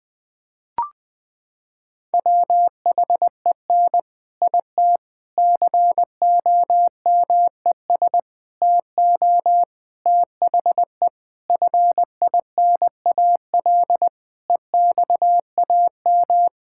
Morse code, heard as a keyed tone, about 21 wpm, 700 Hz. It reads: WHEN IT COMES TO THE FINAL EXAM
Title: Sentences with 5-Word Phrases - 20wpm